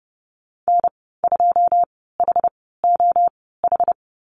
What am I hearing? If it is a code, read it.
Morse code, heard as a keyed tone, about 31 wpm, spaced as for 17 wpm, 700 Hz. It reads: N2HOH
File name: Lesson 15g - Number 5 - Call Signs 30wpm